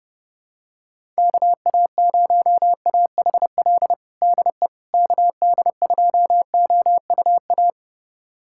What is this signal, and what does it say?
Morse code, heard as a keyed tone, about 28 wpm, 700 Hz. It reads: KA0AHL DE KD2OUA
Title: USA Callsign DE Callsign 30wpm